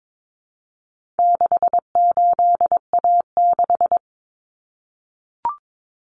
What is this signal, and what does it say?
Morse code, heard as a keyed tone, about 22 wpm, 700 Hz. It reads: 68A6
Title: Common Cut Numbers - 4-Digit 22wpm